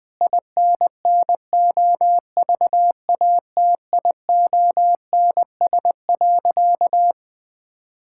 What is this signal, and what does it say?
Morse code, heard as a keyed tone, about 21 wpm, 700 Hz. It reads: INNOVATIONS.
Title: Sentences with 7-Word Phrases - 20wpm